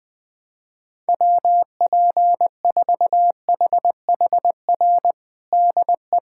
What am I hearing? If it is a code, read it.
Morse code, heard as a keyed tone, about 20 wpm, 700 Hz. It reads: WP4HHR DE
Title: USA Callsign DE Callsign 20wpm